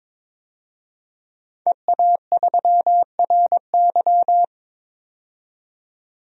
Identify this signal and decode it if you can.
Morse code, heard as a keyed tone, about 22 wpm, 700 Hz. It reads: EA3RY
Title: DXCC-100 Call Signs - 5 Characters - 22wpm